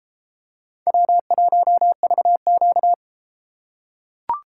Morse code, heard as a keyed tone, about 31 wpm, 700 Hz. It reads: W1VQ